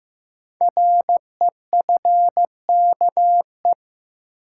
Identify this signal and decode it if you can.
Morse code, heard as a keyed tone, about 15 wpm, 700 Hz. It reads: REFKE
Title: Code Groups - 5 Letter:Numbers - 15wpm